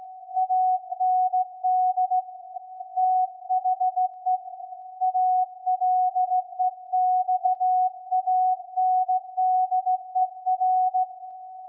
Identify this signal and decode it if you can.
Morse code, heard as a keyed tone, about 17 wpm, 740 Hz. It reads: AND THE ALEXANDER
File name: iz7slz / VLF / SAQ 24-12-2021